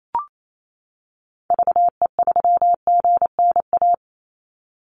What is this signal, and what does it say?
Morse code, heard as a keyed tone, about 28 wpm, 700 Hz. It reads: VE3GNA
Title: Call Signs - 28wpm